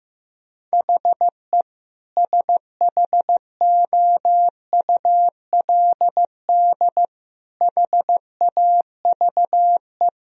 Morse code, heard as a keyed tone, about 15 wpm, 700 Hz. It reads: HE SHOULD HAVE
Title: Sentences from Top 1000 Words - Review Part 3 - 15wpm